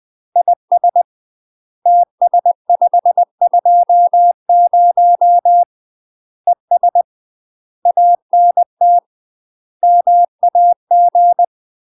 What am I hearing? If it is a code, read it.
Morse code, heard as a keyed tone, about 20 wpm, 700 Hz. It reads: IS TS520 ES ANT MAG
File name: UK MASTER QSOs TRAINING / 20WPM / UK MASTER QSOs 0071